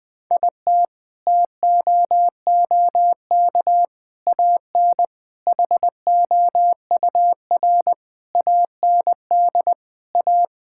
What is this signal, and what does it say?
Morse code, heard as a keyed tone, about 21 wpm, 700 Hz. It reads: IT TOOK AN HOUR AND A
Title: Sentences with 5-Word Phrases - Spoken Beginning - 20wpm